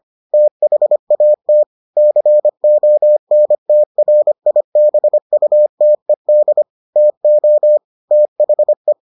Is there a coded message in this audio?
Yes: Morse code from 0.3 to 8.9 s, heard as a keyed tone, about 25 wpm, 600 Hz. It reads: THAT CONTRIBUTED TO THE